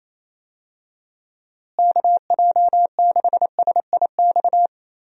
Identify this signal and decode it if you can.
Morse code, heard as a keyed tone, about 27 wpm, 700 Hz. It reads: KJ6SIX
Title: USA Call Signs - 6 Characters - Repeated 2x - 28wpm